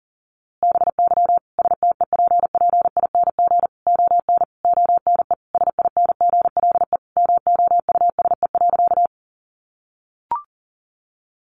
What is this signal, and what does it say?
Morse code, heard as a keyed tone, about 40 wpm, 700 Hz. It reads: BY STEPPING ON ONE SINGLE MOUSE.
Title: A Sound Of Thunder 40wpm